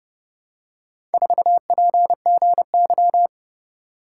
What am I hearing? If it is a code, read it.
Morse code, heard as a keyed tone, about 28 wpm, 700 Hz. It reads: 4PGY